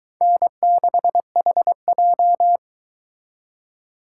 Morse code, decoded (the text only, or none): N6HJ